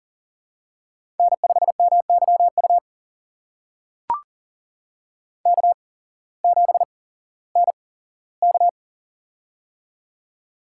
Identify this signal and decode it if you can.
Morse code, heard as a keyed tone, about 39 wpm, spaced as for 19 wpm, 700 Hz. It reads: N5MYU K7NK
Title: USA Call Signs - 8x Character Spacing - 40wpm